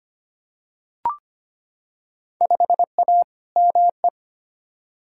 Morse code, heard as a keyed tone, about 25 wpm, 700 Hz. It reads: 5A ME
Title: Field Day Exchanges Encore - 25wpm